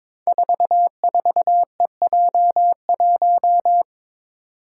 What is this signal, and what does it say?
Morse code, heard as a keyed tone, about 22 wpm, 700 Hz. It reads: 44EJ1
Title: Code Groups - 5 Letter:Numbers - 22wpm